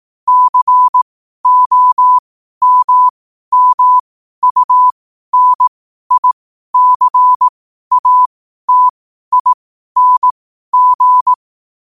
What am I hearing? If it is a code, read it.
Morse code, heard as a keyed tone, about 18 wpm, spaced as for 12 wpm, 1000 Hz. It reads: COMMUNICATING
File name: qotdfiles / 12wpm